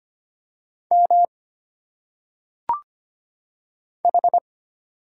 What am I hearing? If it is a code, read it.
Morse code, heard as a keyed tone, about 25 wpm, 700 Hz. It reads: M H
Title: Single Letters - Phonetic Pronunciation 25wpm